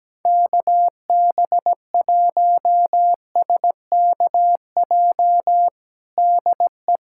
Morse code, heard as a keyed tone, about 17 wpm, 700 Hz. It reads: KB1SKJ DE
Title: USA Callsign DE Callsign 17wpm